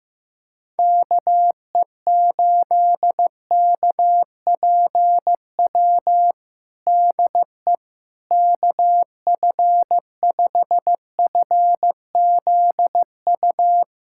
Morse code, heard as a keyed tone, about 15 wpm, 700 Hz. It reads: KE8KPW DE KF5FZU
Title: USA Callsign DE Callsign 15wpm